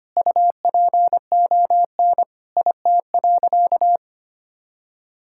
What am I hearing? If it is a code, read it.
Morse code, heard as a keyed tone, about 25 wpm, 700 Hz. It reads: UPON IT.
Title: Sentences from Top 1600 Words 25wpm